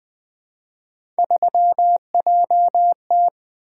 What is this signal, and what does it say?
Morse code, heard as a keyed tone, about 20 wpm, 700 Hz. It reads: 3JT
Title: Three Letter-Number Combinations - 20wpm